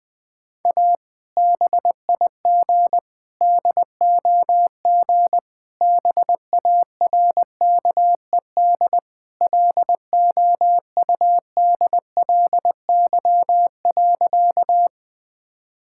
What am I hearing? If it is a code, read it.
Morse code, heard as a keyed tone, about 21 wpm, 700 Hz. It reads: A BIG DOG BARKED LOUDLY.
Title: Sentences with 2-Word Phrases - Spoken Beginning - 20wpm